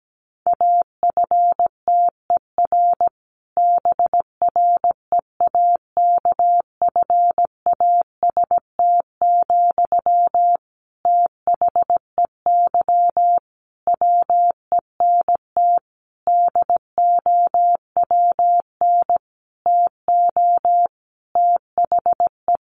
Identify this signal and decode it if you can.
Morse code, heard as a keyed tone, about 17 wpm, 700 Hz. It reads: AFTER BREAKFAST, THEY WENT DOWN TO THE
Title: Sentences from Top 1500 Words 17wpm